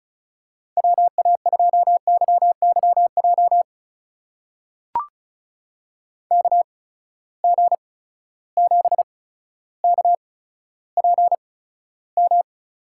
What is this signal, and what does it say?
Morse code, heard as a keyed tone, about 36 wpm, spaced as for 15 wpm, 700 Hz. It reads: WA2YYJ KG7KPM
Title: USA Call Signs - 6 Characters - 8x Character Spacing - 35wpm